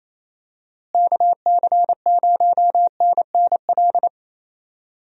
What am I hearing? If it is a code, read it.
Morse code, heard as a keyed tone, about 28 wpm, 700 Hz. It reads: KC0NNL
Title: USA Call Signs - 6 Characters - Repeated 3x - 28wpm